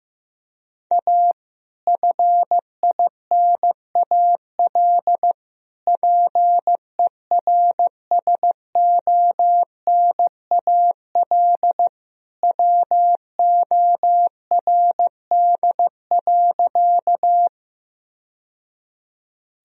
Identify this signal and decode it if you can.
Morse code, heard as a keyed tone, about 16 wpm, 700 Hz. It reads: A FINAL PERSONAL WORD.
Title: Sentences from Top 1000 Words - Review Part 4 - 15wpm